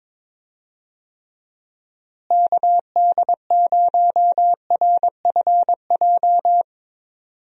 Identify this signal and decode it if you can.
Morse code, heard as a keyed tone, about 22 wpm, 700 Hz. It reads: KD0RFJ
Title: USA Call Signs - 6 Characters - Repeated 3x - 22wpm